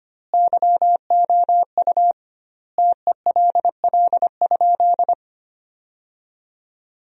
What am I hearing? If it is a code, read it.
Morse code, heard as a keyed tone, about 25 wpm, 700 Hz. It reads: YOU TELL?
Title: 2x Word-Spacing - Sentences from Top 200 Words - 25wpm